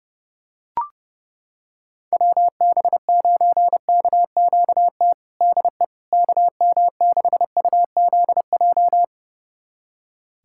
Morse code, heard as a keyed tone, about 30 wpm, 700 Hz. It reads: WB9KQT DE KM6UZJ
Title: USA Callsign DE Callsign - 6 Characters - 30wpm